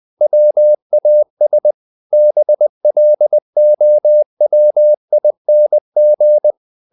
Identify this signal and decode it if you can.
Morse code, heard as a keyed tone, about 20 wpm, 600 Hz. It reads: WAS BLOWING
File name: the old man and the sea part 2 20wpm